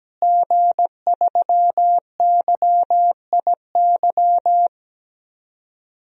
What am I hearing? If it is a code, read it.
Morse code, heard as a keyed tone, about 17 wpm, 700 Hz. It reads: G3YIY
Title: DXCC-100 Call Signs - 5 Characters - Repeated 2x - 17wpm